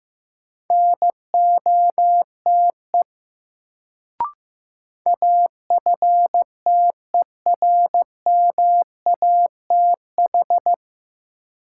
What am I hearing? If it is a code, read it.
Morse code, heard as a keyed tone, about 16 wpm, 700 Hz. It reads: NOTE AFTERMATH
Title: Sets of 5 Words - New Word Drill - 15wpm